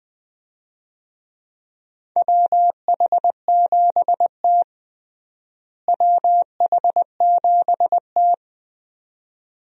Morse code, heard as a keyed tone, about 21 wpm, 700 Hz. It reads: WH7T WH7T
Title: USA Call Signs - 4 Characters - Repeated 2x - 20wpm